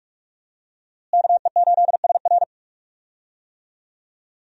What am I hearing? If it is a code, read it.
Morse code, heard as a keyed tone, about 46 wpm, 700 Hz. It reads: KE8SR
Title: USA Call Signs - 5 Characters - Repeated 3x - 45wpm